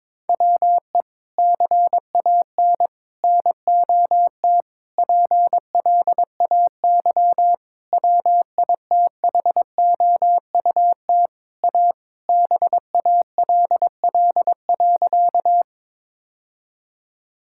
Morse code, heard as a keyed tone, about 23 wpm, 700 Hz. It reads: WE CAN NOT PLAY WITHOUT A BALL.
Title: Sentences from Top 1000 Words - Review Part 6 - 22wpm